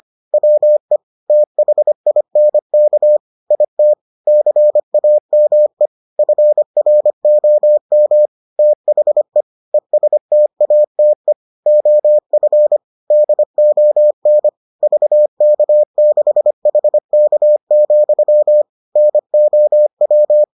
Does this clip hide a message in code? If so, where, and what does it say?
Yes: Morse code from 0.3 to 20.4 s, heard as a keyed tone, about 25 wpm, 600 Hz. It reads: WE THINK IT CAME FROM THE ESTATE OF DON VK6HK, NOW